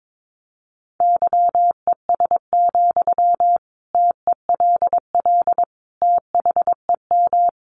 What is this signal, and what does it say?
Morse code, heard as a keyed tone, about 22 wpm, 700 Hz. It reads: YES, TELL THEM
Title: Sentences from Top 500 Words 22wpm